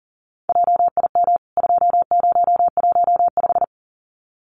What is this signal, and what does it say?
Morse code, heard as a keyed tone, about 37 wpm, 700 Hz. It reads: JIM 2015
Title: CWT Exchange 40wpm